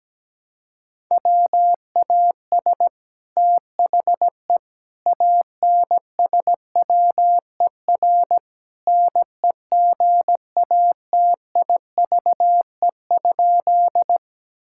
Morse code, heard as a keyed tone, about 17 wpm, 700 Hz. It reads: WAS THE ANSWER NEGATIVE?
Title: Sentences from Top 1000 Words - Review Part 6 - 17wpm